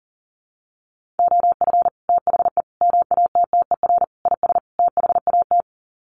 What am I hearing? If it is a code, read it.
Morse code, heard as a keyed tone, about 40 wpm, 700 Hz. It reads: OF THE MATTER IS THAT
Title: Sets of 5 Words Encore 40wpm